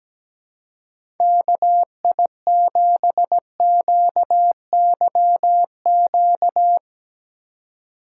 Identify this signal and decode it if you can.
Morse code, heard as a keyed tone, about 17 wpm, 700 Hz. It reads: KI7QYQ
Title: USA Call Signs - 6 Characters - 4x Character Spacing - 17wpm